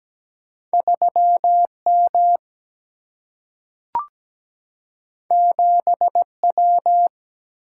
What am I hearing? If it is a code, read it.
Morse code, heard as a keyed tone, about 17 wpm, 700 Hz. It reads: 3M 7W
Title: Single-Letter Single-Number 17wpm